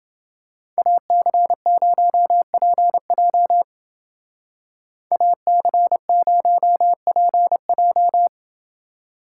Morse code, heard as a keyed tone, about 28 wpm, 700 Hz. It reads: AC0PJ AC0PJ